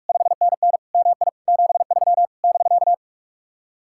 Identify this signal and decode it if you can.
Morse code, heard as a keyed tone, about 58 wpm, 700 Hz. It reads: 5NN MI 73 <BK>